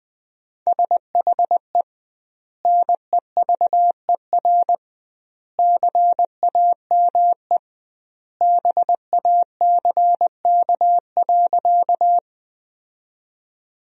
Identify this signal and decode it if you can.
Morse code, heard as a keyed tone, about 20 wpm, 700 Hz. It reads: SHE NEVER CAME BACK.
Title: 2x Word-Spacing - Sentences from Top 300 Words - 20wpm